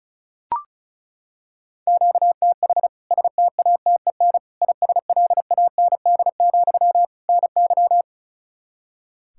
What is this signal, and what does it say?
Morse code, heard as a keyed tone, about 36 wpm, 700 Hz. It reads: QTH STATEN ISLAND, NY
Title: K1USN SST QSO Practice 35wpm